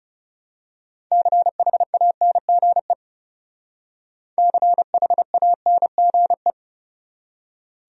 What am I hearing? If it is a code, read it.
Morse code, heard as a keyed tone, about 34 wpm, 700 Hz. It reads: CHANGE CHANGE
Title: Speed-Racing Top 300 Words 30wpm